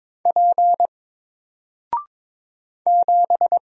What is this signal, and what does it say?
Morse code, heard as a keyed tone, about 22 wpm, 700 Hz. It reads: P 7
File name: Single-Letter-Number-22wpm